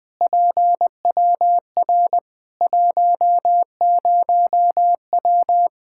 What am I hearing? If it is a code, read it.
Morse code, heard as a keyed tone, about 21 wpm, 700 Hz. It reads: PWR 10W
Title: Practice QSO - Exchange 2 - Info - 20wpm